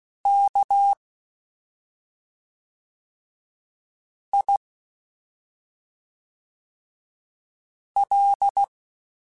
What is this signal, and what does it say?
Morse code, heard as a keyed tone, about 16 wpm, 800 Hz. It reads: K I L
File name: Special INGs / Special INGs-mp3s / SlowToFaster ing @5